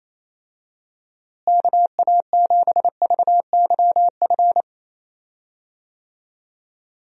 Morse code, heard as a keyed tone, about 27 wpm, 700 Hz. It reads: KA7VYF
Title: USA Call Signs - 6 Characters - Repeated 3x - 28wpm